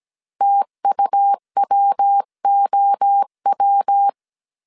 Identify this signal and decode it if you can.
Morse code, heard as a keyed tone, about 17 wpm, 800 Hz. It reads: TUWOW
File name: CW / DET / 1 3 1500C S C 16wpm 2148